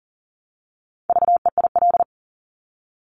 Morse code, heard as a keyed tone, about 41 wpm, 700 Hz. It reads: VEIL
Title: Multi-letter Phonemes Repeated 3x 40wpm